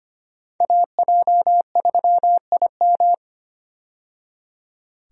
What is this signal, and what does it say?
Morse code, heard as a keyed tone, about 25 wpm, 700 Hz. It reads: AJ3IM
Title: USA Call Signs - 5 Characters - 25wpm